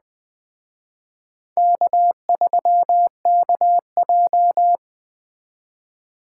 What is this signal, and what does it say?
Morse code, heard as a keyed tone, about 20 wpm, 700 Hz. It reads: K3KJ
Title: Speed Racing - USA 1x2 Call Signs 17wpm